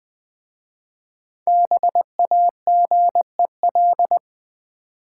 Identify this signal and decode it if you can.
Morse code, heard as a keyed tone, about 20 wpm, 700 Hz. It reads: BAGEL
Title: Easy Words with Hints - 20wpm